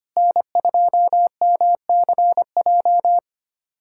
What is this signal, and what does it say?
Morse code, heard as a keyed tone, about 25 wpm, 700 Hz. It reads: N2MCJ
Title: USA Callsign DE Callsign 25wpm